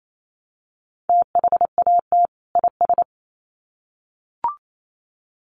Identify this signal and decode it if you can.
Morse code, heard as a keyed tone, about 28 wpm, 700 Hz. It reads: THAT IS
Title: Sets of 2 Words 28wpm